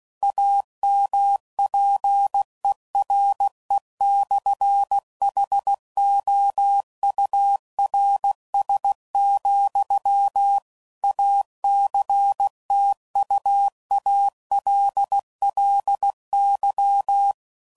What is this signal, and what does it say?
Morse code, heard as a keyed tone, about 16 wpm, 800 Hz. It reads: AMPERE/HOURS, ACTUALLY
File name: CW Code Groups @15WPM / Technical Subjects / PwrSupply